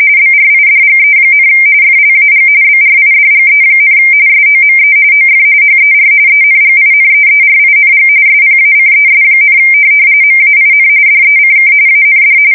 RTTY, about 45.45 baud. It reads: BRIDGE, FLAG BRIDGE, SOME OF HER ENGINEERING SPACES, MAIN DECK AND
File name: fsk170 / USS-New-Jersey